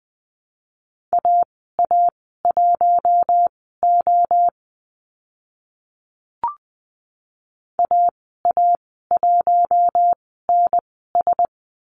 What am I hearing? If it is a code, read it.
Morse code, heard as a keyed tone, about 20 wpm, spaced as for 14 wpm, 700 Hz. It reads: AA1O AA1NS
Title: 048 Lesson 9g - Number 4 - Call Signs 20wpm